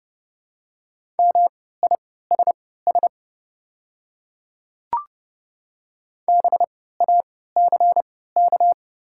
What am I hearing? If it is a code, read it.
Morse code, heard as a keyed tone, about 30 wpm, spaced as for 17 wpm, 690 Hz. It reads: MISS BACK
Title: 3x Character Spacing - Top 500 Words - 30wpm